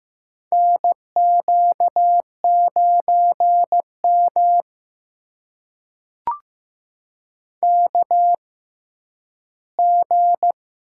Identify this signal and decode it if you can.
Morse code, heard as a keyed tone, about 15 wpm, spaced as for 10 wpm, 700 Hz. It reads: NQ9M KG
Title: USA Call Signs - 6x Character Spacing - 15wpm